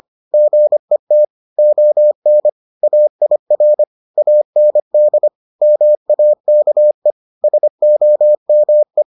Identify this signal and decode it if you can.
Morse code, heard as a keyed tone, about 25 wpm, 600 Hz. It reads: GET ON AIR AND MAKE SOME